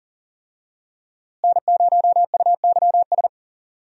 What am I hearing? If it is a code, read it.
Morse code, heard as a keyed tone, about 40 wpm, 700 Hz. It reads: N0UYS
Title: USA Call Signs - 5 Characters - Repeated 3x - 40wpm